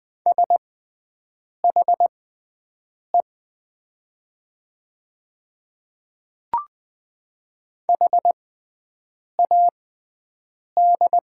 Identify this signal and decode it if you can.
Morse code, heard as a keyed tone, about 20 wpm, spaced as for 7 wpm, 700 Hz. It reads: SHE HAD
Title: 6x Character Spacing - Top 100 Words - 20wpm